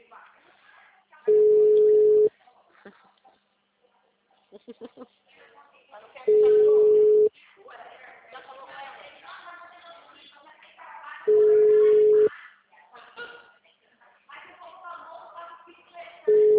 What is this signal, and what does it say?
Ringback tone (Europe, 425 Hz), with speech.
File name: out- -2008- - - .27918